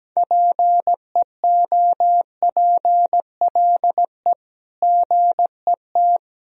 Morse code, heard as a keyed tone, about 17 wpm, 700 Hz. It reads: PEOPLE GET
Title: Sentences from Top 1000 Words - Review Part 1 - 17wpm